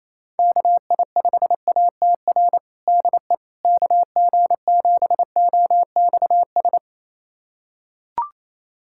Morse code, heard as a keyed tone, about 28 wpm, 700 Hz. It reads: KI5ATR DE KG7OXS
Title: USA Callsign DE Callsign 28wpm